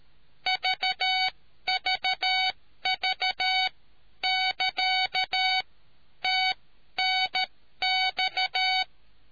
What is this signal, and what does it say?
Morse code, heard as a keyed tone, about 13 wpm, spaced as for 11 wpm, 750 Hz. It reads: VVV <KA> TNX